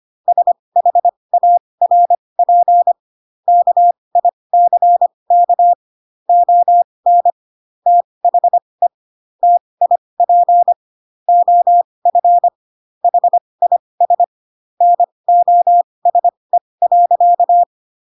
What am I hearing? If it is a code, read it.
Morse code, heard as a keyed tone, about 25 wpm, spaced as for 20 wpm, 700 Hz. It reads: SHARP KICK ON THE TIP OF HIS NOSE.